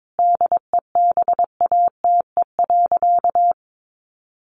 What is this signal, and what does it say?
Morse code, heard as a keyed tone, about 22 wpm, 700 Hz. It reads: DEBATE.
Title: Sentences from Top 1100 Words 22wpm